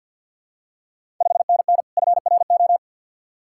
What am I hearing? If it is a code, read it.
Morse code, heard as a keyed tone, about 53 wpm, 700 Hz. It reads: 5NN FRO
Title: Ontario QSO Party Exchange 50wpm